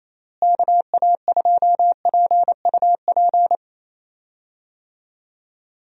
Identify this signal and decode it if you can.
Morse code, heard as a keyed tone, about 28 wpm, 700 Hz. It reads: KA2PUP
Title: USA Call Signs - 6 Characters - Repeated 3x - 28wpm